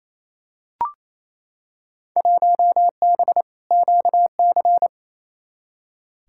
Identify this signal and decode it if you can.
Morse code, heard as a keyed tone, about 28 wpm, 700 Hz. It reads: 1B QC
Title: Field Day Exchanges Encore - 28wpm